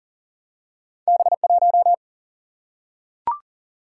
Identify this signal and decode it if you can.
Morse code, heard as a keyed tone, about 41 wpm, 700 Hz. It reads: B1
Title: Single-Letter Single-Number 40wpm